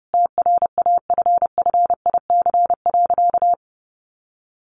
Morse code, heard as a keyed tone, about 28 wpm, 700 Hz. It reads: TRAFFIC.
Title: Sentences from Top 1300 Words 30wpm